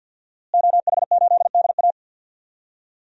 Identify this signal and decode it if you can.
Morse code, heard as a keyed tone, about 52 wpm, 700 Hz. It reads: OH8DA
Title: DXCC-100 Call Signs - 5 Characters - Repeated 2x - 50wpm